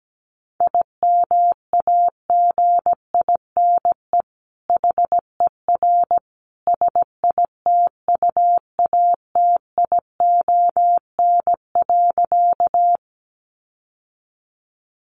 Morse code, heard as a keyed tone, about 17 wpm, 700 Hz. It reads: IMAGINE HER SITUATION.
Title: Sentences from Top 1400 Words 17wpm